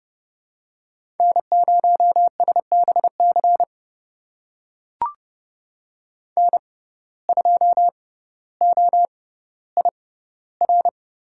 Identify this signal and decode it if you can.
Morse code, heard as a keyed tone, about 30 wpm, spaced as for 15 wpm, 700 Hz. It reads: N0SBC N2OIR
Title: USA Call Signs - 5 Characters - 6x Character Spacing - 30wpm